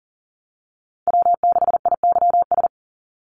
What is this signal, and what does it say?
Morse code, heard as a keyed tone, about 38 wpm, 700 Hz. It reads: W6IYS
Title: NA SOTA Chaser Callsigns 40wpm